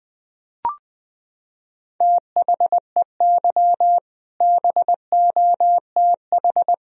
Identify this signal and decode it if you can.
Morse code, heard as a keyed tone, about 20 wpm, 700 Hz. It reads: THEY BOTH
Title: Sentences from Top 1000 Words - Review Part 1 - 20wpm